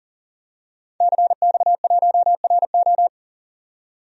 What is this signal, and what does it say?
Morse code, heard as a keyed tone, about 38 wpm, 700 Hz. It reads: CX1RO
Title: DXCC-100 Call Signs - 5 Characters - 40wpm